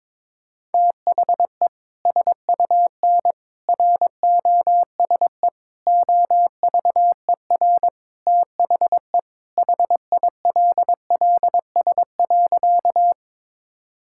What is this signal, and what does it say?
Morse code, heard as a keyed tone, about 22 wpm, 700 Hz. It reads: THE SUN ROSE OVER THE HILLS.
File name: practice / 22wpm / Sentences with 2-Word Phrases - 22wpm